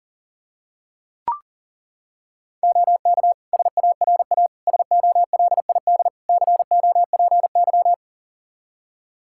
Morse code, heard as a keyed tone, about 40 wpm, 700 Hz. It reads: OK SARA SOLID COPY
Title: Practice QSO - Exchange 3 - More Info - 40wpm